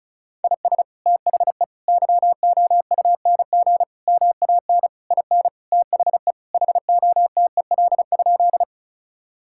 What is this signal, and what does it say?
Morse code, heard as a keyed tone, about 34 wpm, 700 Hz. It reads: IS THE YOUNG MAN IN THE HOTEL?
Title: Sentences from Top 1600 Words 35wpm